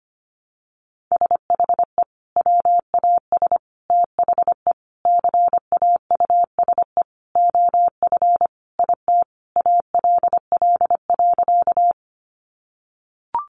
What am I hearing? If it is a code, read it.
Morse code, heard as a keyed tone, about 25 wpm, 700 Hz. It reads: SHE WAS THE CAUSE OF IT ALL.
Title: Sentences from Top 200 Words 25wpm